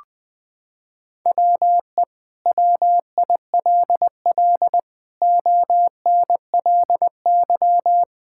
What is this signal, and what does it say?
Morse code, heard as a keyed tone, about 20 wpm, 700 Hz. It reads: WE WILL ONLY
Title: Sentences from Top 1000 Words - Review Part 5 - 20wpm